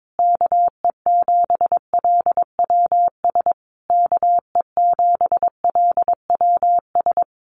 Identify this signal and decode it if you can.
Morse code, heard as a keyed tone, about 22 wpm, 700 Hz. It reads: KE7LWS KE7LWS
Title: Common QSO Phrases Encore 22wpm